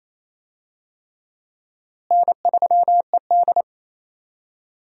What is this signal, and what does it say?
Morse code, heard as a keyed tone, about 28 wpm, 700 Hz. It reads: N3ED
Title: USA Call Signs - 4 Characters - Repeated 3x - 28wpm